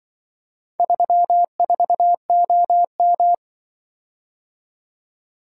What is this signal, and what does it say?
Morse code, heard as a keyed tone, about 24 wpm, 700 Hz. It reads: 34OM
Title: Speed-Racing - Four Letter-Number Combinations - 20wpm